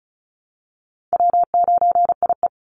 Morse code, heard as a keyed tone, about 34 wpm, 700 Hz. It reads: W9IE
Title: Easy Call Signs Repeated 3x - 35wpm